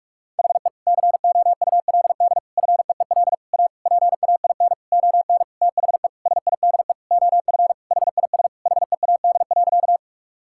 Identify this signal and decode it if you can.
Morse code, heard as a keyed tone, about 41 wpm, 700 Hz. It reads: HE COULD FEEL A PAIN ON THE SIDE OF HIS HEAD.
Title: Sentences from Top 1600 Words 45wpm